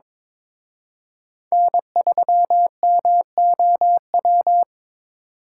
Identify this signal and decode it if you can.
Morse code, heard as a keyed tone, about 27 wpm, 700 Hz. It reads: N3MOW
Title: Speed Racing - USA 1x3 Call Signs 22wpm